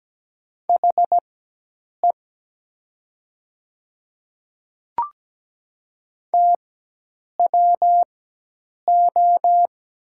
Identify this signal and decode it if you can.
Morse code, heard as a keyed tone, about 17 wpm, spaced as for 8 wpm, 700 Hz. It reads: HE TWO
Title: 4x Character Spacing - Top 100 Words - 17wpm